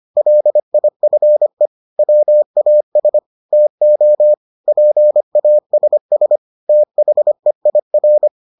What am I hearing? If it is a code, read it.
Morse code, heard as a keyed tone, about 25 wpm, 600 Hz. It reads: LIFE WAS TO PASS THEIR